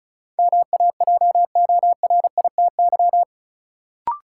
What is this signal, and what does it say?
Morse code, heard as a keyed tone, about 34 wpm, 700 Hz. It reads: MAJORITY
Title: Sets of 3 Words - New Word Drill - 35wpm